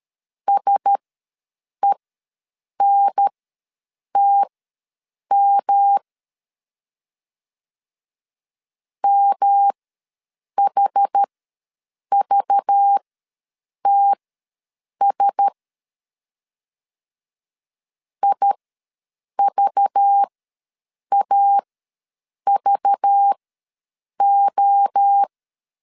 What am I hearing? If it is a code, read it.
Morse code, heard as a keyed tone, about 13 wpm, spaced as for 7 wpm, 800 Hz. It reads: SENTM MHVTS IVAVO
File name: LEZIONI / 5 LBVC / 5 LBV ws 12wpm 3015